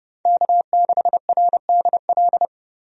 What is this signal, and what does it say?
Morse code, heard as a keyed tone, about 32 wpm, 700 Hz. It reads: K6RDL
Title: USA Call Signs - 5 Characters - Repeated 3x - 30wpm